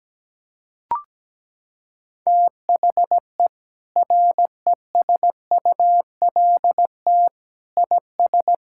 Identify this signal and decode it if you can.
Morse code, heard as a keyed tone, about 17 wpm, 700 Hz. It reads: THE RESULT IS
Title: Sentences from Top 1000 Words - Review Part 5 - 17wpm